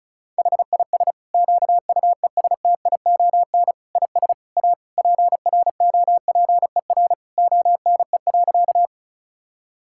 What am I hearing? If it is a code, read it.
Morse code, heard as a keyed tone, about 36 wpm, 700 Hz. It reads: HIS QUESTION IS A PROPER ONE.
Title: Sentences from Top 1000 Words - Review Part 5 - 35wpm